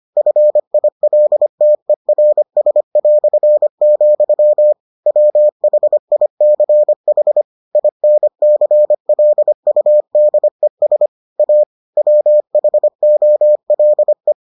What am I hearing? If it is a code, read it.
Morse code, heard as a keyed tone, about 25 wpm, 600 Hz. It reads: FILTERS", WHICH INCLUDES A WHOLE